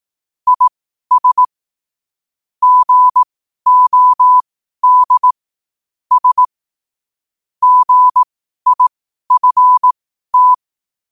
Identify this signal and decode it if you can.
Morse code, heard as a keyed tone, about 18 wpm, spaced as for 12 wpm, 1000 Hz. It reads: IS GOD S GIFT
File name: qotdfiles / 12wpm